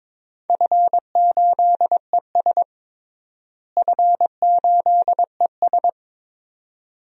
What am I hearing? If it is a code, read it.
Morse code, heard as a keyed tone, about 22 wpm, 700 Hz. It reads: F8ES F8ES
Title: DXCC-100 Call Signs - 4 Characters - Repeated 2x - 22wpm